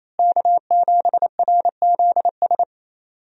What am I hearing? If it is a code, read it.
Morse code, heard as a keyed tone, about 28 wpm, 700 Hz. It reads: K7RZS
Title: USA Call Signs - 5 Characters - Repeated 3x - 28wpm